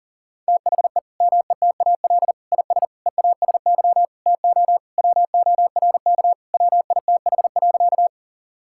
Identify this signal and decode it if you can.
Morse code, heard as a keyed tone, about 40 wpm, 700 Hz. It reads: THE METAL IS EASY TO WORK WITH.
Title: Sentences from Top 1000 Words - Review Part 6 - 40wpm